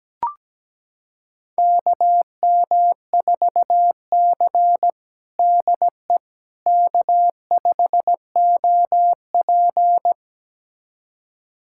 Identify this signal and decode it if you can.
Morse code, heard as a keyed tone, about 17 wpm, 700 Hz. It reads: KM4C DE K5OP
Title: USA Callsign DE Callsign - 4 Characters - 17wpm